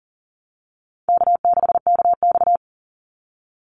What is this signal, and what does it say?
Morse code, heard as a keyed tone, about 40 wpm, 700 Hz. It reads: K6KX
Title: Easy Call Signs Repeated 3x - 40wpm